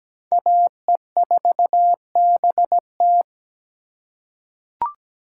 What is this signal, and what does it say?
Morse code, heard as a keyed tone, about 17 wpm, 700 Hz. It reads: AE4BT
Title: USA Call Signs - 5 Characters - 8x Character Spacing - 17wpm